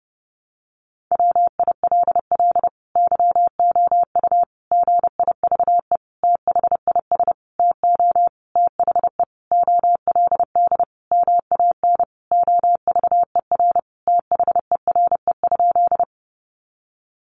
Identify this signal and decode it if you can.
Morse code, heard as a keyed tone, about 30 wpm, 700 Hz. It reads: WILL YOU GIVE THIS TO THE OLD MAN OVER THERE?
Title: Sentences from Top 200 Words 30wpm